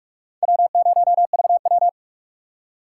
Morse code, heard as a keyed tone, about 56 wpm, 700 Hz. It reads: W0VW
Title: Speed Racing - USA 1x2 Call Signs 45wpm